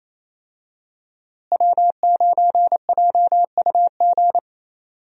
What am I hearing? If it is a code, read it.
Morse code, heard as a keyed tone, about 28 wpm, 700 Hz. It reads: W9JUG
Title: USA Call Signs - 5 Characters - Repeated 3x - 28wpm